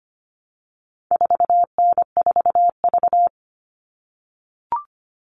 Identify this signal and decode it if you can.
Morse code, heard as a keyed tone, about 25 wpm, 700 Hz. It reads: 4N4V
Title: All Cut Numbers - 4-Digit 25wpm